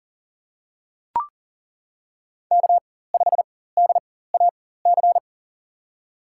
Morse code, heard as a keyed tone, about 40 wpm, spaced as for 19 wpm, 700 Hz. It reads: K5DAC
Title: USA Call Signs - 5 Characters - 4x Character Spacing - 40wpm